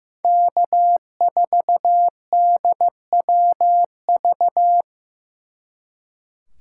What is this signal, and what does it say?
Morse code, heard as a keyed tone, about 18 wpm, 700 Hz. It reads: K4DWV
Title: Speed Racing - USA 1x3 Call Signs 15wpm